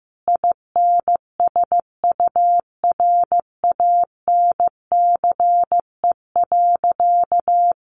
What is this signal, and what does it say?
Morse code, heard as a keyed tone, about 15 wpm, 700 Hz. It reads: INSURANCE.
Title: Sentences from Top 1100 Words 15wpm